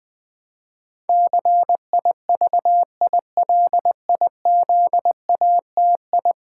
Morse code, heard as a keyed tone, about 20 wpm, 700 Hz. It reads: CIVILIZATI
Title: Build A Word - 5 Letters At A Time - 12-Letter Words - 20wpm